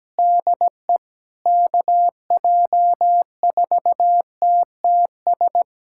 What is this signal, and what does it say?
Morse code, heard as a keyed tone, about 17 wpm, 700 Hz. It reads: DE KJ4TTS
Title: USA Callsign DE Callsign - 6 Characters - 17wpm